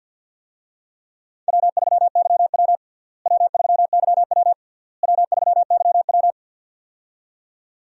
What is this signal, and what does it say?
Morse code, heard as a keyed tone, about 53 wpm, 700 Hz. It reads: W3YW W3YW W3YW
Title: USA Call Signs - 4 Characters - Repeated 3x - 50wpm